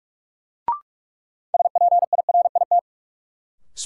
Morse code, heard as a keyed tone, about 46 wpm, 700 Hz. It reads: SPIRIT
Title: Sentences from Top 1400 Words - New Word Drill - 45wpm